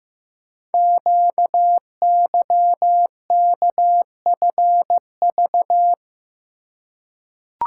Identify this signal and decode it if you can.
Morse code, heard as a keyed tone, about 15 wpm, 700 Hz. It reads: QYKFV
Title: Code Groups - 5 Letters - 15wpm